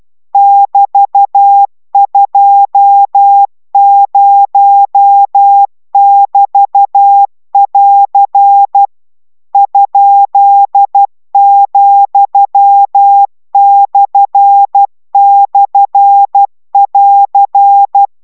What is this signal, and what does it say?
Morse code, heard as a keyed tone, about 12 wpm, 800 Hz. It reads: <BT>20<BT><AR> ?,//<AR>